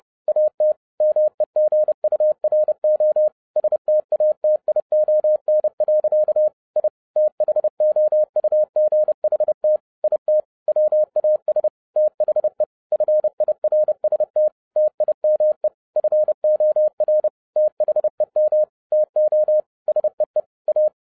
Morse code, heard as a keyed tone, about 30 wpm, 610 Hz. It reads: AT MEGURO STATION. I THOUGHT IT WAS THE FIRST TIME FOR THEM TO SEE A